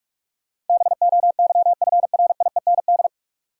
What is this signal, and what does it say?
Morse code, heard as a keyed tone, about 46 wpm, 700 Hz. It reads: BOYFRIEND
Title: Sentences from Top 1000 Words - Review Part 5 - New Word Drill - 45wpm